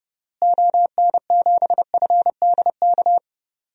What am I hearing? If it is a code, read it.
Morse code, heard as a keyed tone, about 32 wpm, 700 Hz. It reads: ON7FDK
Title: DXCC-100 Call Signs - 6 Characters - Repeated 3x - 30wpm